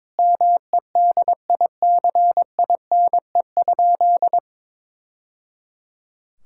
Morse code, heard as a keyed tone, about 22 wpm, 700 Hz. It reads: MEDICINE?
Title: Sentences from Top 1000 Words - Review Part 6 - 22wpm